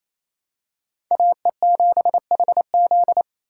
Morse code, heard as a keyed tone, about 28 wpm, 700 Hz. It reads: AE7HZ
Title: USA Call Signs - 5 Characters - 4x Character Spacing - 28wpm